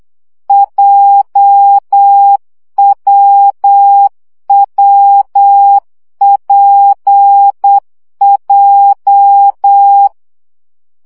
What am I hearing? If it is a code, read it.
Morse code, heard as a keyed tone, about 8 wpm, 800 Hz. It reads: JWWPJ
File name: CW gyakorlas w p j 7 baud